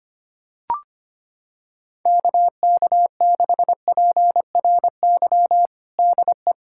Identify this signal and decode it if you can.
Morse code, heard as a keyed tone, about 25 wpm, 700 Hz. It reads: KK6PRY DE
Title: USA Callsign DE Callsign - 6 Characters - 25wpm